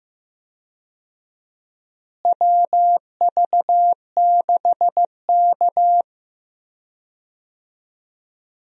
Morse code, heard as a keyed tone, about 15 wpm, 700 Hz. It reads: WV6K
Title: USA Call Signs - 4 Characters - Repeated 2x - 15wpm